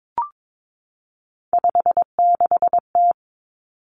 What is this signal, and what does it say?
Morse code, heard as a keyed tone, about 22 wpm, 700 Hz. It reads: 56T
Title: Three Challenging Letter-Number Combinations 22wpm